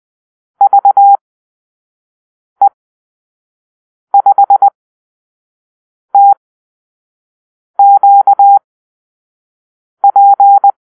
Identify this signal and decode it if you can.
Morse code, heard as a keyed tone, about 20 wpm, spaced as for 5 wpm, 800 Hz. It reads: VE5TQP
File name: arrl / Call-Sign Drill-1